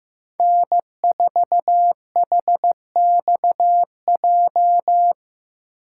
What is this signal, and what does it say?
Morse code, heard as a keyed tone, about 15 wpm, 700 Hz. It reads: N4HXJ
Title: USA Call Signs - 5 Characters - 6x Character Spacing - 15wpm